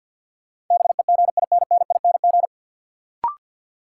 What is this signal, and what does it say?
Morse code, heard as a keyed tone, about 53 wpm, 700 Hz. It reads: BEGINNING
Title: Sentences from Top 800 Words - New Word Drill - 50wpm